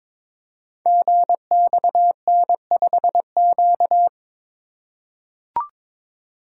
Morse code, heard as a keyed tone, about 22 wpm, 700 Hz. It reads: GXN5Q
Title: Code Groups - 5 Letter:Numbers - 22wpm